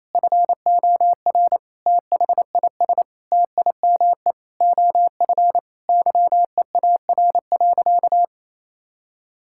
Morse code, heard as a keyed tone, about 28 wpm, 700 Hz. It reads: FOR THIS TIME OF YEAR.
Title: Sentences from Top 1000 Words - Review Part 6 - 28wpm